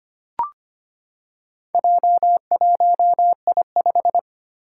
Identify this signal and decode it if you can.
Morse code, heard as a keyed tone, about 25 wpm, 700 Hz. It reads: J1I5
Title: Four Challenging Letter-Number Combinations 25wpm